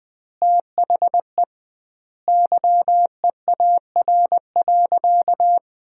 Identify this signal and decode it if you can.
Morse code, heard as a keyed tone, about 20 wpm, 700 Hz. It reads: THE YEAR.
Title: 2x Word-Spacing - Sentences from Top 300 Words - 20wpm